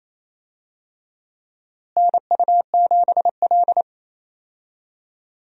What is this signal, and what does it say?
Morse code, heard as a keyed tone, about 28 wpm, 700 Hz. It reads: NU7L
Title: USA Call Signs - 4 Characters - Repeated 3x - 28wpm